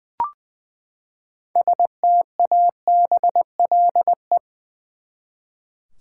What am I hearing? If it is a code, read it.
Morse code, heard as a keyed tone, about 20 wpm, 700 Hz. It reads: STABLE
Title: Six Letter Words - 20wpm